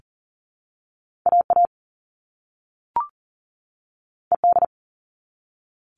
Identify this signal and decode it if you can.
Morse code, heard as a keyed tone, about 40 wpm, 700 Hz. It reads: AA ED
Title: All Cut Numbers - 2-Digit 40wpm